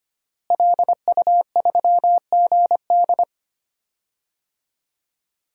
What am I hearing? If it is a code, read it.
Morse code, heard as a keyed tone, about 25 wpm, 700 Hz. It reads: LU3GD
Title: DXCC-100 Call Signs - 5 Characters - Repeated 2x - 25wpm